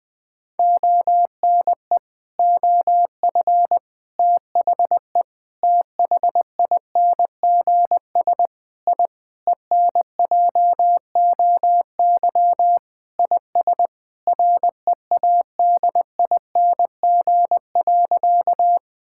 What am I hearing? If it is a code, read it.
Morse code, heard as a keyed tone, about 20 wpm, 700 Hz. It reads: ONE OF THE THINGS I ENJOY IS READING.
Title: Sentences with 5-Word Phrases - Spoken Beginning - 20wpm